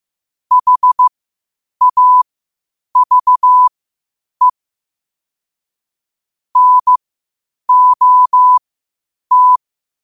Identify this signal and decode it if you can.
Morse code, heard as a keyed tone, about 15 wpm, spaced as for 8 wpm, 1000 Hz. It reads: HAVE NOT
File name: qotdfiles / 8wpm